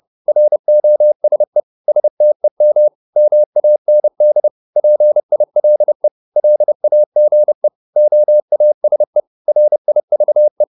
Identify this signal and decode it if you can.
Morse code, heard as a keyed tone, about 30 wpm, 600 Hz. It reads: ROSE STEM MAND PILE LAGE OASE RIVE
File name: Korte ord DK 30wpm